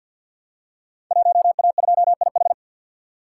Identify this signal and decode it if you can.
Morse code, heard as a keyed tone, about 52 wpm, 700 Hz. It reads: 1A2IH
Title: Code Groups - 5 Letter:Numbers - 50wpm